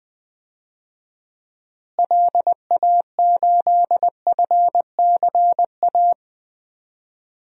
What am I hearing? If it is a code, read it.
Morse code, heard as a keyed tone, about 20 wpm, 700 Hz. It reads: LA8FCA
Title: DXCC-100 Call Signs - 6 Characters - Repeated 2x - 20wpm